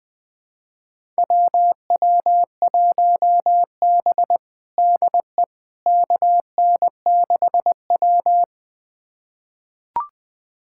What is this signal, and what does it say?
Morse code, heard as a keyed tone, about 20 wpm, 700 Hz. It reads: WW1B DE KN6W
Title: USA Callsign DE Callsign - 4 Characters - 20wpm